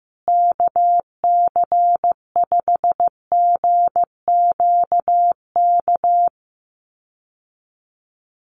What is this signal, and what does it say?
Morse code, heard as a keyed tone, about 15 wpm, 700 Hz. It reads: KC5GQK
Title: USA Call Signs Repeated 2x 15wpm